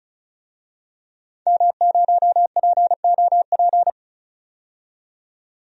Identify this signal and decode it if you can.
Morse code, heard as a keyed tone, about 36 wpm, 700 Hz. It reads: M0POP
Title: DXCC-100 Call Signs - 5 Characters - 35wpm